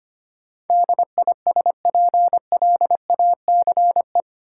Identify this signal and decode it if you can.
Morse code, heard as a keyed tone, about 25 wpm, 700 Hz. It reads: DISPLACE
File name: Eight Letter Words - 25wpm